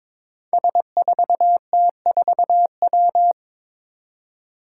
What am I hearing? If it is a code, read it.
Morse code, heard as a keyed tone, about 22 wpm, 700 Hz. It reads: S4T4W
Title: Five Challenging Letter-Number Combinations 22wpm